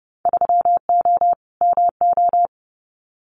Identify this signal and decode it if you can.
Morse code, heard as a keyed tone, about 30 wpm, 700 Hz. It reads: 3O MO
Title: Winter Field Day Exchanges Encore 30wpm